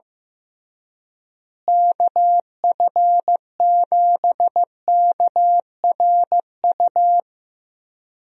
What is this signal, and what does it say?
Morse code, heard as a keyed tone, about 15 wpm, 700 Hz. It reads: KF7KRU
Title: USA Call Signs - 6 Characters - Repeated 3x - 15wpm